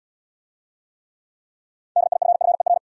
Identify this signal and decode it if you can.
Morse code, heard as a keyed tone, about 69 wpm, 700 Hz. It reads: DIFFER